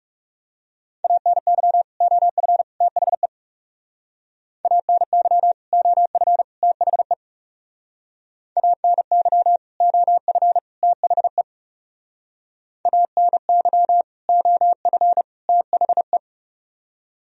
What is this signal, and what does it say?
Morse code, heard as a keyed tone, about 38 wpm, 700 Hz. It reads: ANY OF THE ANY OF THE ANY OF THE ANY OF THE